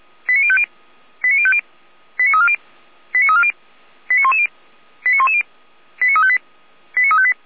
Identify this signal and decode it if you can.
Five-tone selective call (ZVEI), heard as quick runs of five tones, tones of about 70 ms. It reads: 89050 89050 89340 89340 89110 89110 89358 89358